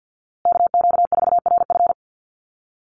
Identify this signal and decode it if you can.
Morse code, heard as a keyed tone, about 50 wpm, 700 Hz. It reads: KQ4RF
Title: USA Call Signs Repeated 3x 50wpm